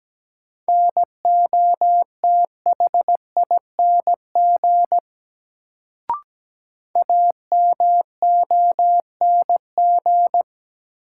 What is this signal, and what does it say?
Morse code, heard as a keyed tone, about 17 wpm, 700 Hz. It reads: NOTHING AMONG
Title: Sentences from Top 500 Words - New Word Drill - 17wpm